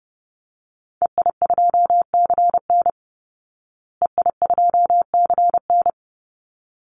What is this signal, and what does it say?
Morse code, heard as a keyed tone, about 30 wpm, 700 Hz. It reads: EI2CN EI2CN
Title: Call Signs Repeated 3x - 30wpm